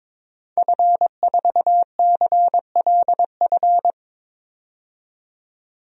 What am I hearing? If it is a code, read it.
Morse code, heard as a keyed tone, about 22 wpm, 700 Hz. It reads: F4CLF
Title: DXCC-100 Call Signs - 5 Characters - 22wpm